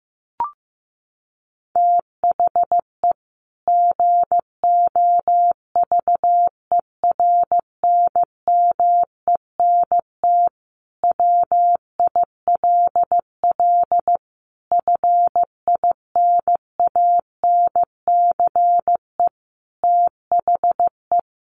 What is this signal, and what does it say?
Morse code, heard as a keyed tone, about 15 wpm, 700 Hz. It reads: THE GOVERNMENT WILL FINANCE THE
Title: Sentences from Top 1500 Words 15wpm